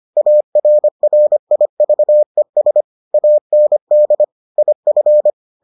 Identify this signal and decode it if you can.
Morse code, heard as a keyed tone, about 25 wpm, 600 Hz. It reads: ARRIVES AND IF